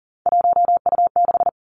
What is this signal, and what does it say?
Morse code, heard as a keyed tone, about 40 wpm, 700 Hz. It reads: 1U6
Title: Three Challenging Letter-Number Combinations 40wpm